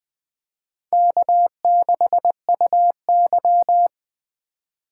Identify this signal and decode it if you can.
Morse code, heard as a keyed tone, about 20 wpm, 700 Hz. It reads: K6UY
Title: USA Call Signs - 4x Character Spacing - 20wpm